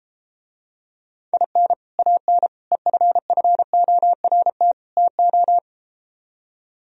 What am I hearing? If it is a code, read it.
Morse code, heard as a keyed tone, about 33 wpm, 700 Hz. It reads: IN AN EFFORT TO